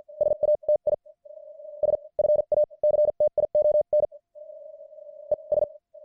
Morse code, heard as a keyed tone, about 50 wpm, spaced as for 30 wpm, 600 Hz. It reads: SATISFACTION ES
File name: archived / CWBros